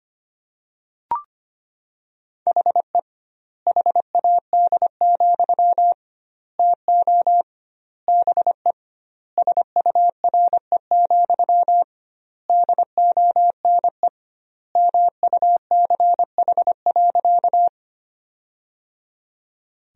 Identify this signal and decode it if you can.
Morse code, heard as a keyed tone, about 25 wpm, 700 Hz. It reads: HE HAD, TO BE SURE, DONE MUCH.
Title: 2x Word-Spacing - Sentences from Top 500 Words - 25wpm